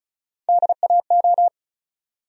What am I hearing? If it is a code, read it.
Morse code, heard as a keyed tone, about 36 wpm, 700 Hz. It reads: DAO
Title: USA Call Sign Postfixes - 3 Letters 35wpm